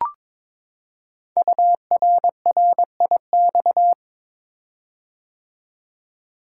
Morse code, heard as a keyed tone, about 21 wpm, 700 Hz. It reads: URRIX
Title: Code Groups - 5 Letters - 22wpm